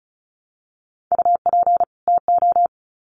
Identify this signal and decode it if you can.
Morse code, heard as a keyed tone, about 34 wpm, 700 Hz. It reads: UP TO
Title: Sets of 2 Words 35wpm